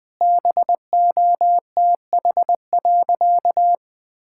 Morse code, heard as a keyed tone, about 20 wpm, 700 Hz. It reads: BOTH.
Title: Sentences from Top 1000 Words - Review Part 6 - 20wpm